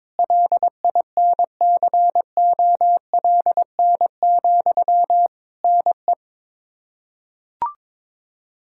Morse code, heard as a keyed tone, about 22 wpm, 700 Hz. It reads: LINCOLN, NE
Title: Practice QSO - Exchange 1 - Beginning - 22wpm